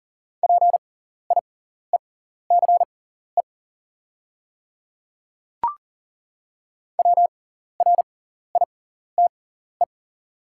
Morse code, heard as a keyed tone, about 40 wpm, spaced as for 14 wpm, 700 Hz. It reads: PIECE WRITE
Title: 6x Character Spacing - Top 500 Words - 40wpm